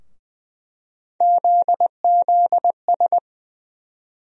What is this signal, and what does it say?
Morse code, heard as a keyed tone, about 20 wpm, 700 Hz. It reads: ZZS
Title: USA Call Sign Postfixes - 3 Letters 20wpm